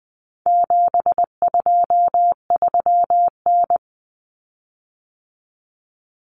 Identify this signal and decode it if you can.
Morse code, heard as a keyed tone, about 20 wpm, 700 Hz. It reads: 723N
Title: Common Cut Numbers - 4-Digit 20wpm